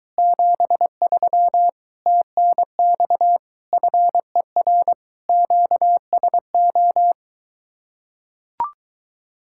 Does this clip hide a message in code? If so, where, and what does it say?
Yes: Morse code from 0.2 to 7.1 s, heard as a keyed tone, about 23 wpm, 700 Hz. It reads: 73 TNX FER QSO